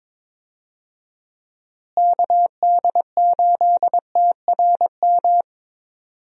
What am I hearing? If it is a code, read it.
Morse code, heard as a keyed tone, about 22 wpm, 700 Hz. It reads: KD8TRM
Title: USA Call Signs - 6 Characters - Repeated 2x - 22wpm